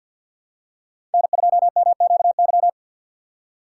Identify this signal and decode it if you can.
Morse code, heard as a keyed tone, about 53 wpm, 700 Hz. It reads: N2MQY
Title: USA Call Signs - 5 Characters - 3x Character Spacing - 50wpm